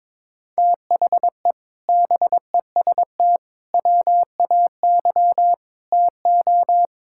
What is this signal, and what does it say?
Morse code, heard as a keyed tone, about 22 wpm, 700 Hz. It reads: THE BEST WAY TO
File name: practice / 22wpm / Sentences with 4-Word Phrases - 22wpm